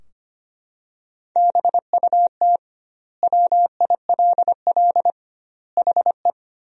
Morse code, heard as a keyed tone, about 25 wpm, 700 Hz. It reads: BUT WILL HE
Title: 2x Word-Spacing - Sentences from Top 100 Words - 25wpm